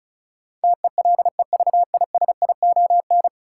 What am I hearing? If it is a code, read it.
Morse code, heard as a keyed tone, about 36 wpm, 700 Hz. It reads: TELEVISION
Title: Sentences from Top 1000 Words - Review Part 4 - New Word Drill - 35wpm